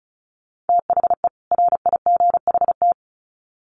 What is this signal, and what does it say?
Morse code, heard as a keyed tone, about 34 wpm, 700 Hz. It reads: THE RIGHT
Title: Sets of 2 Words 35wpm